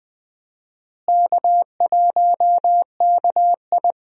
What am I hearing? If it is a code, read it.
Morse code, heard as a keyed tone, about 20 wpm, 700 Hz. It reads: K1KI
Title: USA Call Signs - 4 Characters - Repeated 2x - 20wpm